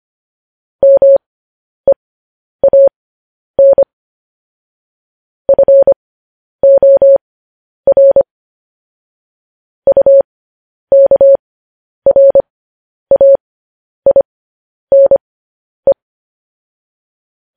Morse code, heard as a keyed tone, about 25 wpm, spaced as for 10 wpm, 550 Hz. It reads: MEAN FOR UKRAINE
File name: podcast / News headlines-1x-25x10